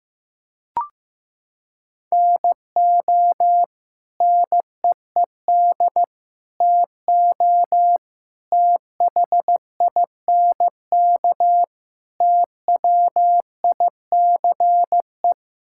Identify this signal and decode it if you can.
Morse code, heard as a keyed tone, about 16 wpm, 700 Hz. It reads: NO NEED TO THINK TWICE
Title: Sentences from Top 1600 Words 15wpm